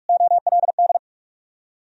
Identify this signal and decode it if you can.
Morse code, heard as a keyed tone, about 60 wpm, spaced as for 53 wpm, 700 Hz. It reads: OLD
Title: Speed-Racing - Sets of 2 Words 45wpm